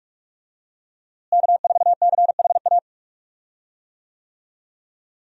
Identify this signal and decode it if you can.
Morse code, heard as a keyed tone, about 46 wpm, 700 Hz. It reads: K4CHA
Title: USA Call Signs - 5 Characters - Repeated 2x - 45wpm